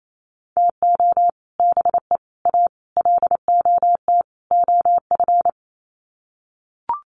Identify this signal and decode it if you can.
Morse code, heard as a keyed tone, about 28 wpm, 700 Hz. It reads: TO BE A LOT OF
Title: Sets of 5 Words Encore 28wpm